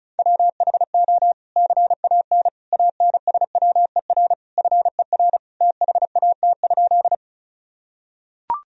Morse code, heard as a keyed tone, about 34 wpm, 700 Hz. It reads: WHO CAN ANSWER FER THAT?
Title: Sentences from Top 300 Words and Abbreviations 35wpm